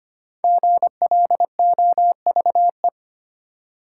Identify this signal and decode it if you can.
Morse code, heard as a keyed tone, about 25 wpm, 690 Hz. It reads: GLOVE
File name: practice / 25wpm / Easy Words with Hints - 25wpm